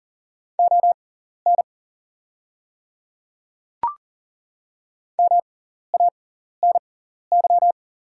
Morse code, heard as a keyed tone, about 40 wpm, spaced as for 14 wpm, 690 Hz. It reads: ON MANY
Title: 6x Character Spacing - Top 300 Words - 40wpm